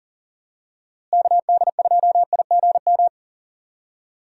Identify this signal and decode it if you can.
Morse code, heard as a keyed tone, about 40 wpm, 700 Hz. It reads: KD2IGM
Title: USA Call Signs - 6 Characters - Repeated 2x - 40wpm